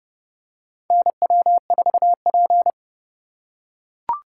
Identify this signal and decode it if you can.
Morse code, heard as a keyed tone, about 31 wpm, 700 Hz. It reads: NW4P
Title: USA Call Signs - 4 Characters - 3x Character Spacing - 30wpm